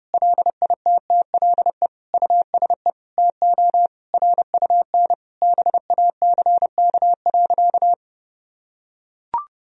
Morse code, heard as a keyed tone, about 30 wpm, 700 Hz. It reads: LITTLE USE TO RUN BACK.
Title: Sentences from Top 1000 Words - Review Part 1 - 30wpm